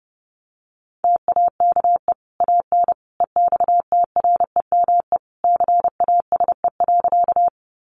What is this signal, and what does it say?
Morse code, heard as a keyed tone, about 30 wpm, 700 Hz. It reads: TAKE AN EXTREME CASE.
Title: Sentences from Top 1300 Words 30wpm